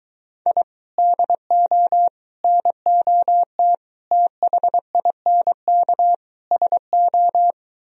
Morse code, heard as a keyed tone, about 23 wpm, 700 Hz. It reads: I DO NOT THINK SO